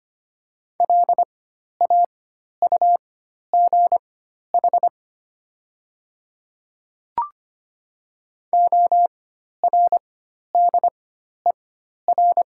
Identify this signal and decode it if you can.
Morse code, heard as a keyed tone, about 25 wpm, spaced as for 12 wpm, 700 Hz. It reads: LAUGH ORDER
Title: 4x Character Spacing - Top 500 Words - 25wpm